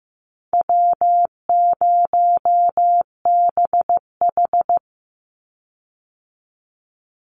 Morse code, heard as a keyed tone, about 15 wpm, 700 Hz. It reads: W0BH
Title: Easy Call Signs - 15wpm